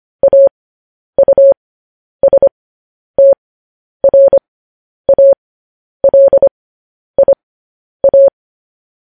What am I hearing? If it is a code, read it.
Morse code, heard as a keyed tone, about 25 wpm, spaced as for 10 wpm, 550 Hz. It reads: AUSTRALIA
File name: podcast / News headlines-1x-25x10